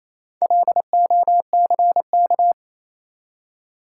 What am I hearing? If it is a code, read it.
Morse code, heard as a keyed tone, about 27 wpm, 700 Hz. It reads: LOCK
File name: Easy Words - Speed-Racing with Hints - 22wpm